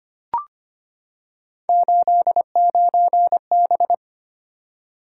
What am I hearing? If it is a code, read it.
Morse code, heard as a keyed tone, about 25 wpm, 700 Hz. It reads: 89B
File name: Three Letter-Number Combinations - 25wpm